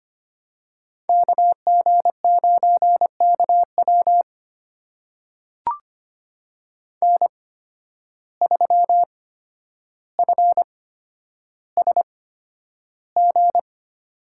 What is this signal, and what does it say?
Morse code, heard as a keyed tone, about 25 wpm, spaced as for 11 wpm, 700 Hz. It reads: KG9KW N3FSG
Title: USA Call Signs - 5 Characters - 8x Character Spacing - 25wpm